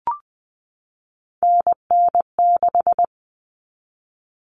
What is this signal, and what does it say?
Morse code, heard as a keyed tone, about 20 wpm, 700 Hz. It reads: NN6
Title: Common Cut Numbers - 3-Digit 20wpm